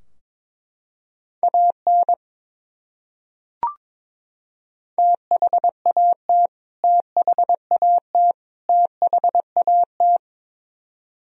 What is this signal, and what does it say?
Morse code, heard as a keyed tone, about 21 wpm, 700 Hz. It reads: AN THAT THAT THAT
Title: Top 100 Words Repeated 22wpm